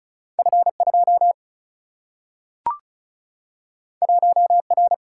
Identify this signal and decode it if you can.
Morse code, heard as a keyed tone, about 36 wpm, 700 Hz. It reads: F2 1R
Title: Single-Letter Single-Number 35wpm